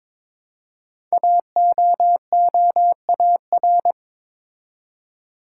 Morse code, heard as a keyed tone, about 22 wpm, 700 Hz. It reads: AOOAR
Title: Code Groups - 5 Letter:Numbers - 22wpm